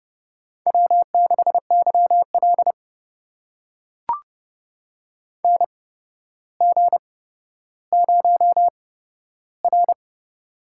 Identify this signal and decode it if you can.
Morse code, heard as a keyed tone, about 30 wpm, spaced as for 13 wpm, 700 Hz. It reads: W6YL NG0R
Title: USA Call Signs - 4 Characters - 8x Character Spacing - 30wpm